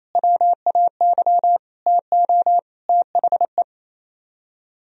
Morse code, heard as a keyed tone, about 34 wpm, 700 Hz. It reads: WAY TO THE
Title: Speed-Racing - Sets of 5 Words 28wpm